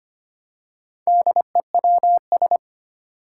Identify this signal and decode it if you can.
Morse code, heard as a keyed tone, about 25 wpm, 700 Hz. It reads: DEWS
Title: Four Challenging Letter-Number Combinations 25wpm